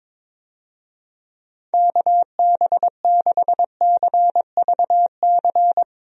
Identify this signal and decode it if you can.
Morse code, heard as a keyed tone, about 21 wpm, 700 Hz. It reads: KB6CVC
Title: USA Call Signs - 6 Characters - Repeated 2x - 22wpm